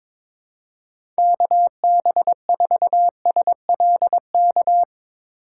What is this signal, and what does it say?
Morse code, heard as a keyed tone, about 22 wpm, 700 Hz. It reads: KB4SLK
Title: USA Call Signs - 6 Characters - 4x Character Spacing - 22wpm